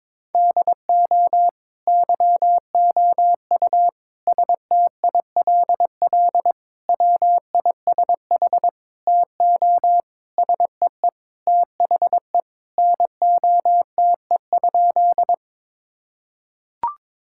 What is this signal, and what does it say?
Morse code, heard as a keyed tone, about 22 wpm, 700 Hz. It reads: DO YOU STILL WISH TO SEE THE NOTE?
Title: Sentences from Top 1600 Words 22wpm